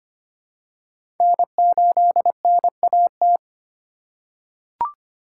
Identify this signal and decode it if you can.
Morse code, heard as a keyed tone, about 25 wpm, 700 Hz. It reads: N8NAT
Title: USA Call Signs - 5 Characters - 3x Character Spacing - 25wpm